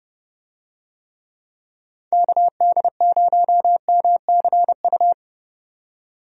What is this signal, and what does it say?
Morse code, heard as a keyed tone, about 30 wpm, 700 Hz. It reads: KD0MCU
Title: USA Call Signs - 6 Characters - 30wpm